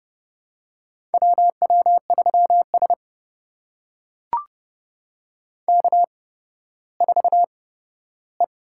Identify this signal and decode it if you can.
Morse code, heard as a keyed tone, about 30 wpm, spaced as for 15 wpm, 700 Hz. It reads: WW3S K4E
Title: USA Call Signs - 4 Characters - 8x Character Spacing - 30wpm